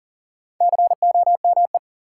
Morse code, heard as a keyed tone, about 37 wpm, 700 Hz. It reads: COME
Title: Sets of 2 Words - New Word Drill - 40wpm